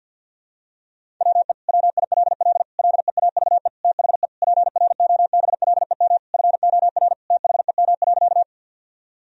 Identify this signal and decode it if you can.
Morse code, heard as a keyed tone, about 52 wpm, 700 Hz. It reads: WE WILL LEAVE THE PROBLEM FOR THEM.
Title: Sentences from Top 1000 Words - Review Part 2 - 50wpm